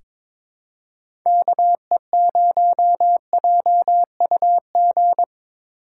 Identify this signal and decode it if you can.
Morse code, heard as a keyed tone, about 22 wpm, 700 Hz. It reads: KE0JUG
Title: USA Call Signs - 6 Characters - 3x Character Spacing - 22wpm